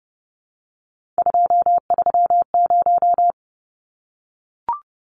Morse code, heard as a keyed tone, about 30 wpm, 700 Hz. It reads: 230
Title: Numbers - 3-Digit 30wpm